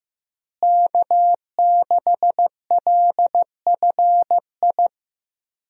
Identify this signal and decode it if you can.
Morse code, heard as a keyed tone, about 15 wpm, 700 Hz. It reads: K6LFI
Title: USA Call Signs - 5 Characters - Repeated 3x - 15wpm